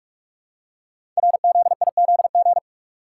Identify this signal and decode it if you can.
Morse code, heard as a keyed tone, about 44 wpm, 700 Hz. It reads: RZIZG
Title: Code Groups - 5 Letters - 45wpm